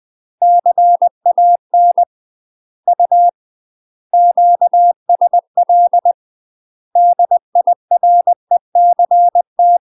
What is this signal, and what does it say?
Morse code, heard as a keyed tone, about 20 wpm, 700 Hz. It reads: CAN U QSL DIRECT
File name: UK MASTER QSOs TRAINING / 20WPM / UK MASTER QSOs 0050